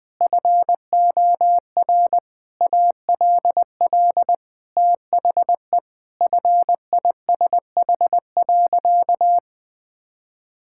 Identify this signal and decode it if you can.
Morse code, heard as a keyed tone, about 21 wpm, 700 Hz. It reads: FOR ALL THE FISH.
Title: Sentences with 8-Word Phrases - 20wpm